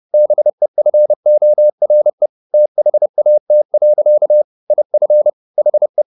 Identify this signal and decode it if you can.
Morse code, heard as a keyed tone, about 30 wpm, 600 Hz. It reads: BEFORE THAT. IF HE
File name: the old man and the sea / the old man and the sea 1 30wpm